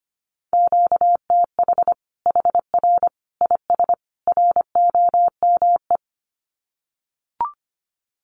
Morse code, heard as a keyed tone, about 25 wpm, 700 Hz. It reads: QTH HR IS ROME
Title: Common QSO Phrases 25wpm